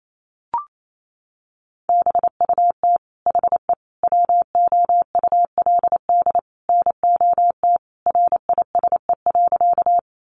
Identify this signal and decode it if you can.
Morse code, heard as a keyed tone, about 28 wpm, 700 Hz. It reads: BUT HE WOULD NOT RISE.
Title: Sentences from Top 900 Words 28wpm